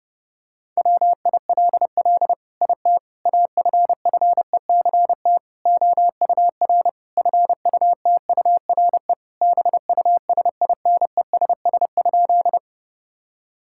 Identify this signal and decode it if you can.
Morse code, heard as a keyed tone, about 30 wpm, 700 Hz. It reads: WILL IT AFFECT OUR FUTURE BUSINESS?
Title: Sentences from Top 1000 Words - Review Part 6 - 30wpm